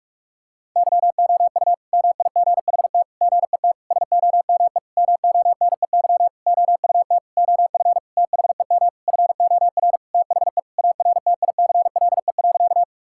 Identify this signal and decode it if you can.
Morse code, heard as a keyed tone, about 46 wpm, 700 Hz. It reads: YOU MIGHT GET SOME MONEY OUT OF THEM FOR THE ARTICLE.
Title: Sentences from Top 1000 Words - Review Part 5 - 45wpm